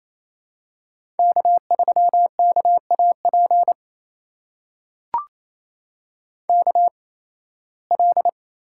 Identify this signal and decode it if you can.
Morse code, heard as a keyed tone, about 28 wpm, spaced as for 18 wpm, 700 Hz. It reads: K3KAP KL
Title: USA Call Signs - 5 Characters - 8x Character Spacing - 28wpm